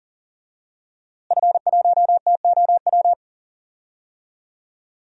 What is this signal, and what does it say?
Morse code, heard as a keyed tone, about 40 wpm, 700 Hz. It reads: F1TOW
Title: DXCC-100 Call Signs - 5 Characters - Repeated 2x - 40wpm